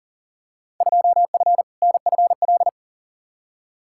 Morse code, heard as a keyed tone, about 40 wpm, 700 Hz. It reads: 2F NFL
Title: Field Day Exchanges - 40wpm